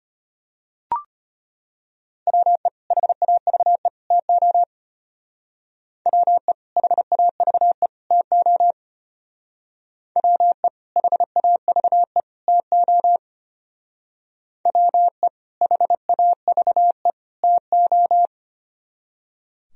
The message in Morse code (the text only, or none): WE HAVE TO WE HAVE TO WE HAVE TO WE HAVE TO